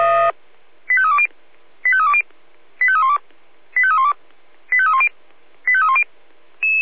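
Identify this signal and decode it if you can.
Five-tone selective call (ZVEI), heard as quick runs of five tones, tones of about 70 ms. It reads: 85320 85320 85312 85312 85310 85310
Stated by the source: ZVEI five-tone selective call received on an SDR.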